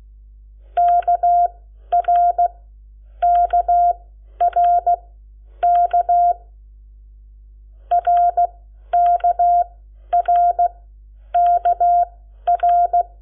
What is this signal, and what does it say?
Morse code, heard as a keyed tone, about 16 wpm, spaced as for 10 wpm, 700 Hz. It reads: KRKRK RKRKR
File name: Lecciones / CLASE Nº7